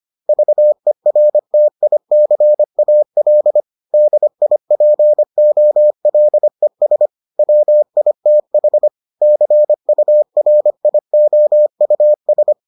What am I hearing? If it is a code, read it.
Morse code, heard as a keyed tone, about 25 wpm, 600 Hz. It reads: VERTICAL DIPOLES WITH CURIOUS